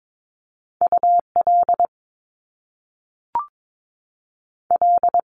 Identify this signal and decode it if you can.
Morse code, heard as a keyed tone, about 22 wpm, spaced as for 3 wpm, 700 Hz. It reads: U LL
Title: Top Two Letter Combinations in Words 22wpm